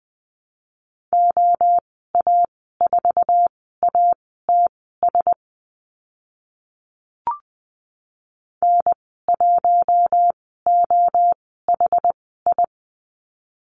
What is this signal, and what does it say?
Morse code, heard as a keyed tone, about 20 wpm, spaced as for 15 wpm, 700 Hz. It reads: OA4ATS N1OHI
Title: 069 Lesson 12g - Letter d - Call Signs 20wpm